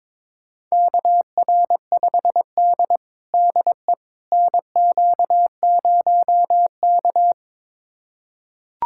Morse code, heard as a keyed tone, about 22 wpm, 700 Hz. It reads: KR5D DE NQ0K
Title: USA Callsign DE Callsign - 4 Characters - 22wpm